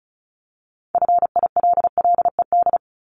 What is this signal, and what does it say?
Morse code, heard as a keyed tone, about 34 wpm, 700 Hz. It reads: FILLED
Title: Multi-letter Phonemes Repeated 3x 35wpm